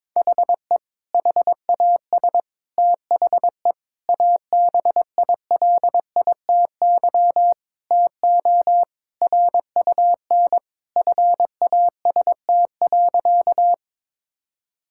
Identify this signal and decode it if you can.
Morse code, heard as a keyed tone, about 22 wpm, 700 Hz. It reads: HE HAS THE ABILITY TO RUN FAST.
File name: practice / 22wpm / Sentences with 3-Word Phrases - 22wpm